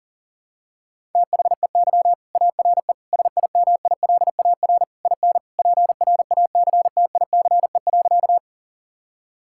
Morse code, heard as a keyed tone, about 37 wpm, 700 Hz. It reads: THEY ARE SIMILAR IN PRACTICE.
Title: Sentences from Top 1000 Words - Review Part 5 - 40wpm